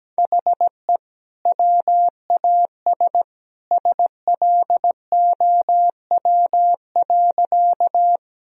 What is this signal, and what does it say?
Morse code, heard as a keyed tone, about 17 wpm, 700 Hz. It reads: HE WAS SLOW.
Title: Sentences from Top 1000 Words - Review Part 2 - 17wpm